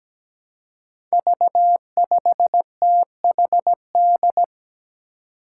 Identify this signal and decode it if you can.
Morse code, heard as a keyed tone, about 17 wpm, 700 Hz. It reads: V5THD
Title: Five Challenging Letter-Number Combinations 17wpm